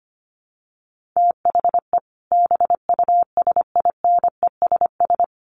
Morse code, heard as a keyed tone, about 25 wpm, 700 Hz. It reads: THE BUSINESS
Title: Sentences from Top 1500 Words 25wpm